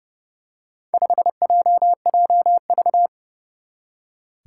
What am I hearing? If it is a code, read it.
Morse code, heard as a keyed tone, about 30 wpm, 700 Hz. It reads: 5JJV
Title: Four Challenging Letter-Number Combinations 30wpm